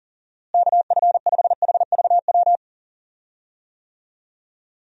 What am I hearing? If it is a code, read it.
Morse code, heard as a keyed tone, about 40 wpm, 700 Hz. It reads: KF5HVW
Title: USA Call Signs - 6 Characters - Repeated 3x - 40wpm